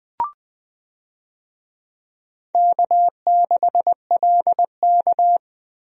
Morse code, heard as a keyed tone, about 20 wpm, 700 Hz. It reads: K6LK
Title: USA Call Signs - 4 Characters - Repeated 3x - 20wpm